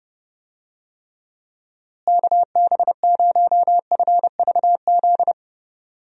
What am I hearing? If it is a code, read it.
Morse code, heard as a keyed tone, about 29 wpm, 700 Hz. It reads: KB0FVZ
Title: USA Call Signs - 6 Characters - Repeated 3x - 30wpm